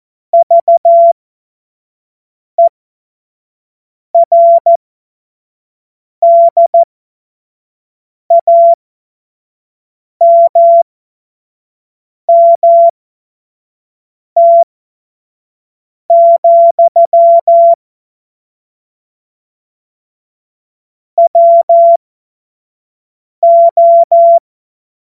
Morse code, heard as a keyed tone, about 14 wpm, spaced as for 5 wpm, 680 Hz. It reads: VERDAMMT, WO
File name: Morsekurs / Witze / Witz 16 April 2024 25 F70